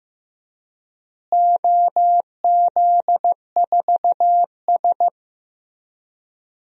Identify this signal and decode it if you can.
Morse code, heard as a keyed tone, about 16 wpm, 700 Hz. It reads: OZ4S
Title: DXCC-100 Call Signs - 4 Characters - Repeated 3x - 15wpm